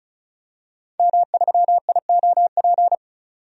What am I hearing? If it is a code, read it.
Morse code, heard as a keyed tone, about 34 wpm, 700 Hz. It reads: M3IOP
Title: DXCC-100 Call Signs - 5 Characters - 35wpm